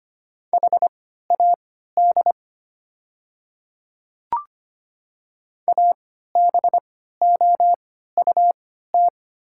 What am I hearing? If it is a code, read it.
Morse code, heard as a keyed tone, about 25 wpm, spaced as for 14 wpm, 690 Hz. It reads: HAD ABOUT
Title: 3x Character Spacing - Top 100 Words - 25wpm